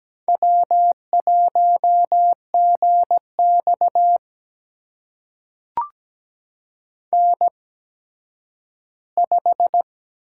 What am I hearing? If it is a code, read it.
Morse code, heard as a keyed tone, about 17 wpm, spaced as for 10 wpm, 700 Hz. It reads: W1GX N5
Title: USA Call Signs - 4 Characters - 8x Character Spacing - 17wpm